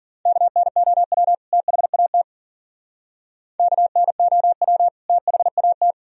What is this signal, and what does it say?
Morse code, heard as a keyed tone, about 52 wpm, 700 Hz. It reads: KNOW THAT KNOW THAT